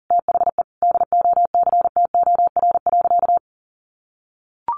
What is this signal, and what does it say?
Morse code, heard as a keyed tone, about 40 wpm, 700 Hz. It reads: THE DOCTOR.
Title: Sentences from Top 1500 Words 40wpm